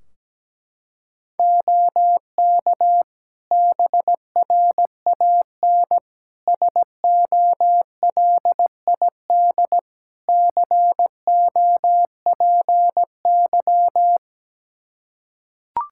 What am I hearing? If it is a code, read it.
Morse code, heard as a keyed tone, about 17 wpm, 700 Hz. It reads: OK BRAN SOLID COPY
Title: Practice QSO - Exchange 3 - More Info - 17wpm